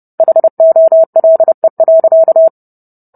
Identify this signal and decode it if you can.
Morse code, heard as a keyed tone, about 30 wpm, 650 Hz. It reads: HOLE.